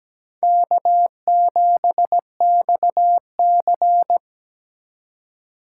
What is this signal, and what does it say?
Morse code, heard as a keyed tone, about 20 wpm, 700 Hz. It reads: K7XC
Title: Speed Racing - USA 1x2 Call Signs 17wpm